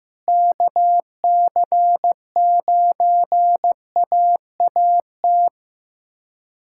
Morse code, heard as a keyed tone, about 16 wpm, 700 Hz. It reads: KC9AAT
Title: USA Callsign DE Callsign - 6 Characters - 15wpm